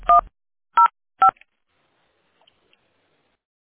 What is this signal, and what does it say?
DTMF, heard as telephone touch tones, tones of about 110 ms. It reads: 105